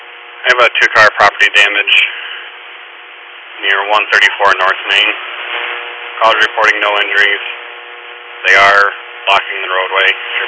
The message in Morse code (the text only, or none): none